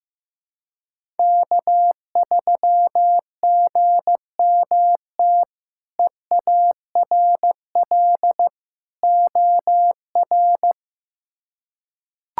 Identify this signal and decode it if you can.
Morse code, heard as a keyed tone, about 15 wpm, 700 Hz. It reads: K3GMT EARL OR
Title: K1USN SST QSO Practice - 15wpm